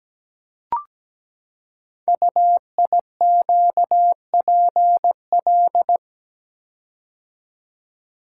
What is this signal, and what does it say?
Morse code, heard as a keyed tone, about 17 wpm, 700 Hz. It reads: UIQPL
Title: Code Groups - 5 Letters - 17wpm